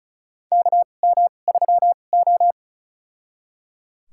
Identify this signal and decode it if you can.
Morse code, heard as a keyed tone, about 36 wpm, spaced as for 25 wpm, 700 Hz. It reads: KM3O
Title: USA Call Signs - 4 Characters - 2x Character Spacing - 35wpm